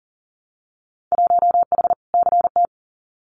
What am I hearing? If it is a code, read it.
Morse code, heard as a keyed tone, about 40 wpm, 700 Hz. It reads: 1H CT
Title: Winter Field Day Exchanges 40wpm